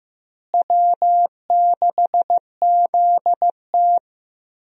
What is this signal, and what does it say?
Morse code, heard as a keyed tone, about 15 wpm, 700 Hz. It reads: W6ZT
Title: USA Call Signs - 4 Characters - Repeated 2x - 15wpm